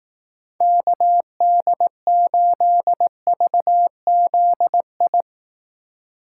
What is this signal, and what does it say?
Morse code, heard as a keyed tone, about 18 wpm, 700 Hz. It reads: KD8VZI
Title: Speed Racing - USA 2x3 Call Signs 15wpm